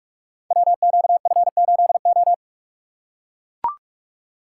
Morse code, heard as a keyed tone, about 44 wpm, 700 Hz. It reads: WQF8O
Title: Code Groups - 5 Letter:Numbers - 45wpm